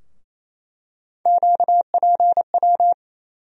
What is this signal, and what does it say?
Morse code, heard as a keyed tone, about 28 wpm, 700 Hz. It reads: QPW
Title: USA Call Sign Postfixes - 3 Letters 28wpm